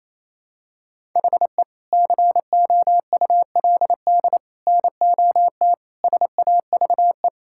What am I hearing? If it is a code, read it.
Morse code, heard as a keyed tone, about 28 wpm, 700 Hz. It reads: HE COULD NOT SAVE
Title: Sentences from Top 1600 Words 28wpm